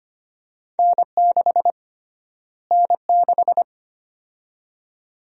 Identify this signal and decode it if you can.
Morse code, heard as a keyed tone, about 25 wpm, 700 Hz. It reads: N6 N6
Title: USA Call Sign Prefixes - Repeated 2x - 25wpm